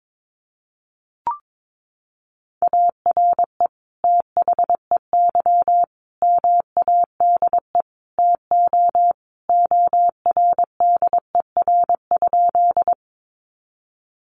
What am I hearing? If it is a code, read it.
Morse code, heard as a keyed tone, about 22 wpm, 700 Hz. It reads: ARE THEY MADE TO ORDER?
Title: Sentences from Top 400 Words 22wpm